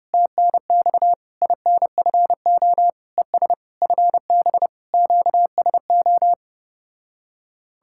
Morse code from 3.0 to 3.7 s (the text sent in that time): ES